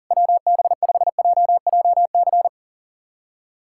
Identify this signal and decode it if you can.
Morse code, heard as a keyed tone, about 40 wpm, 700 Hz. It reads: WB5JJC
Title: USA Callsign DE Callsign 40wpm